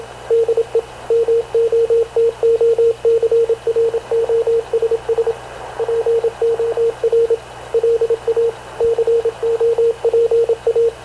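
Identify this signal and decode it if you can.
Morse code, heard as a keyed tone, about 26 wpm, 460 Hz, with radio noise. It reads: DE MOTOCROSS POR LA COPA
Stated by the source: Havana Radio / CLA sending Spanish press (PX) in Morse, recorded by Radio Officer Paul Dumesnil onboard CNAV QUEST/CZDO.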